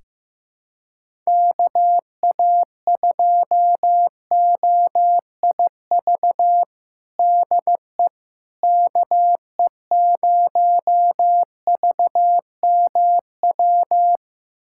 Morse code, heard as a keyed tone, about 16 wpm, 700 Hz. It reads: KA2OIV DE KE0VMW
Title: USA Callsign DE Callsign - 6 Characters - 15wpm